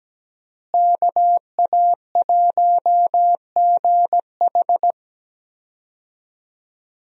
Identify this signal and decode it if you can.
Morse code, heard as a keyed tone, about 17 wpm, 700 Hz. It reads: KA1GH
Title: USA Call Signs - 5 Characters - Repeated 3x - 17wpm